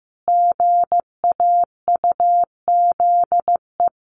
Morse code, heard as a keyed tone, about 15 wpm, 700 Hz. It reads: GAUZE
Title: A Sound Of Thunder 15wpm